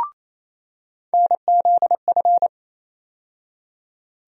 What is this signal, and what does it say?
Morse code, heard as a keyed tone, about 28 wpm, 700 Hz. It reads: NZF
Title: USA Call Sign Postfixes - 3 Letters 28wpm